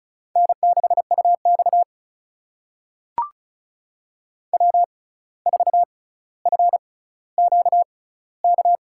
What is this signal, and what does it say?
Morse code, heard as a keyed tone, about 36 wpm, spaced as for 17 wpm, 700 Hz. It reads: N6UX W4FQK
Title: USA Call Signs - 6x Character Spacing - 35wpm